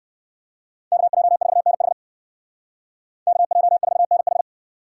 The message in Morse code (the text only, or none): KJ4NH KJ4NH